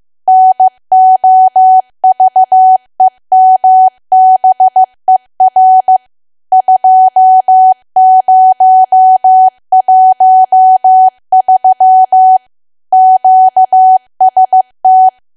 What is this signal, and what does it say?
Morse code, heard as a keyed tone, about 15 wpm, 750 Hz. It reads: NOVEMBER 2013 QST
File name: cwa / W1AW-Level2 / 15WPM